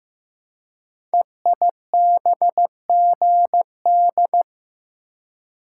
Morse code, heard as a keyed tone, about 15 wpm, 700 Hz. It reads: EIBGD
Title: Code Groups - 5 Letters - 15wpm